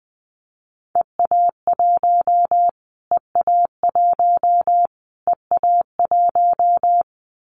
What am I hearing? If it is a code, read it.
Morse code, heard as a keyed tone, about 20 wpm, 700 Hz. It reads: EA1 EA1 EA1
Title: Callsign Prefixes for Top 25 DXCC - Repeated 3x - 20wpm